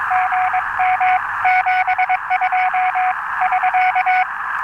Morse code, heard as a keyed tone, about 22 wpm, 740 Hz, with radio noise. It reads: GM 72 <SK>
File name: attachments / 5.03.2015 RV3GM 14060